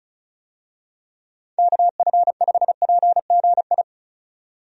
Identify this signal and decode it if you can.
Morse code, heard as a keyed tone, about 34 wpm, 700 Hz. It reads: KF5PGI
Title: USA Call Signs - 6 Characters - 35wpm